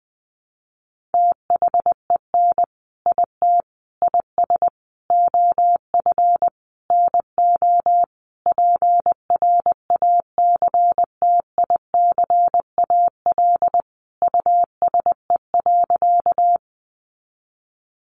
Morse code, heard as a keyed tone, about 20 wpm, 700 Hz. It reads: THEN IT IS OF NO PRACTICAL USE.
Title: Sentences from Top 1400 Words 20wpm